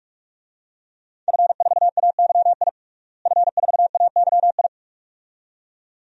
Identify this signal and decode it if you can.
Morse code, heard as a keyed tone, about 43 wpm, 700 Hz. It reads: F4AYI F4AYI
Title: DXCC-100 Call Signs - 5 Characters - Repeated 2x - 45wpm